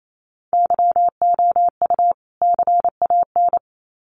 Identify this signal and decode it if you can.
Morse code, heard as a keyed tone, about 28 wpm, 700 Hz. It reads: YOU CAN
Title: Sets of 2 Words 28wpm